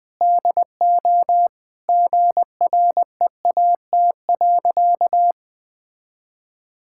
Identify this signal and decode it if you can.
Morse code, heard as a keyed tone, about 20 wpm, 690 Hz. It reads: DO GREAT.
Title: Sentences with 4-Word Phrases - Spoken Beginning - 20wpm